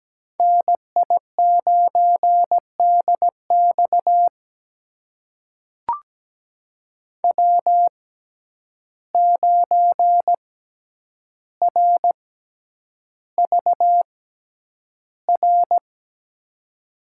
Morse code, heard as a keyed tone, about 17 wpm, spaced as for 9 wpm, 700 Hz. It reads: NI9DX W9RVR
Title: USA Call Signs - 5 Characters - 6x Character Spacing - 17wpm